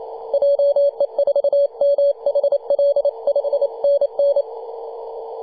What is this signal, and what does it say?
Morse code, heard as a keyed tone, about 28 wpm, 570 Hz, with radio noise. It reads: JE4MHL 5NN
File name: 2015/May/15 2222z PW0F 21.026MHz CW